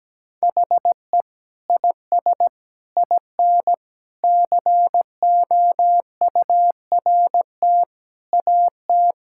Morse code, heard as a keyed tone, about 17 wpm, 700 Hz. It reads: HE IS IN COURT AT
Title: Sentences from Top 1600 Words 17wpm